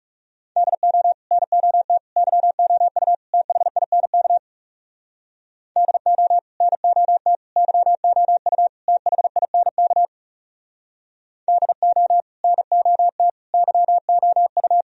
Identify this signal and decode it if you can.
Morse code, heard as a keyed tone, about 39 wpm, 700 Hz. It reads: DO NOT YOU THINK DO NOT YOU THINK DO NOT YOU